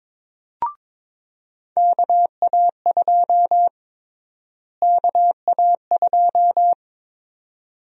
Morse code, heard as a keyed tone, about 22 wpm, 700 Hz. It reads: KA2 KA2
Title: USA Call Sign Prefixes - Repeated 2x - 22wpm